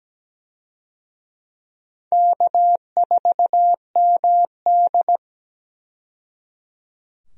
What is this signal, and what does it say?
Morse code, heard as a keyed tone, about 17 wpm, 700 Hz. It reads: K4MD
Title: USA Call Signs - 4 Characters - 17wpm